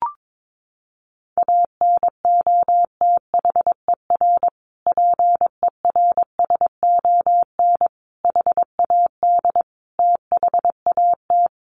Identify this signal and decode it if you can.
Morse code, heard as a keyed tone, about 22 wpm, 700 Hz. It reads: ANOTHER PERSON HAD THAT
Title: Sentences from Top 1400 Words 22wpm